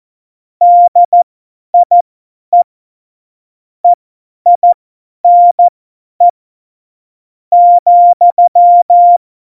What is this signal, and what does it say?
Morse code, heard as a keyed tone, about 14 wpm, spaced as for 10 wpm, 700 Hz. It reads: DIE EINE ,
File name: Morsekurs / Witze / Witz 21. August 2018, 50 F70